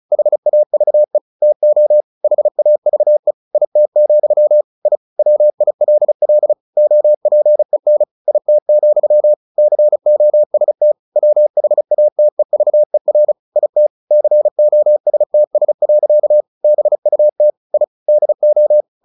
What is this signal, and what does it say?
Morse code, heard as a keyed tone, about 34 wpm, 600 Hz. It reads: HAVE TO HAVE IT, I WILL OPEN IT, COST WHATEVER IT COSTS. BUT I DO
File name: the old man and the sea / the old man and the sea 1 35wpm